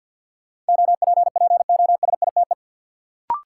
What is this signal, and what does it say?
Morse code, heard as a keyed tone, about 52 wpm, 700 Hz. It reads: OPPOSITE
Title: Sentences from Top 1300 Words - New Word Drill - 50wpm